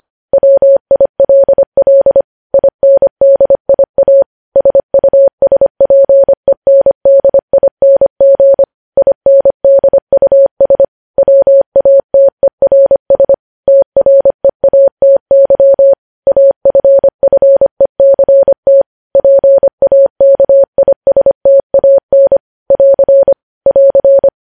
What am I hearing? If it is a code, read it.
Morse code, heard as a keyed tone, about 25 wpm, 550 Hz. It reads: WILL INDIA SUSPENDING INDUS WATERS TREATY AFFECT PAKISTAN <AR> <AR>
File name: podcast / News headlines-1x-25x25